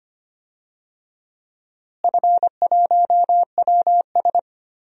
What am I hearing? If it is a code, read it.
Morse code, heard as a keyed tone, about 25 wpm, 700 Hz. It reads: F1WS
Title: DXCC-100 Call Signs - 4 Characters - Repeated 2x - 25wpm